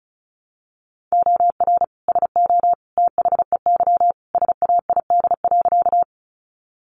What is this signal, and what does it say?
Morse code, heard as a keyed tone, about 34 wpm, 700 Hz. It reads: OR SO THEY SAID.
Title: Sentences from Top 100 Words 35wpm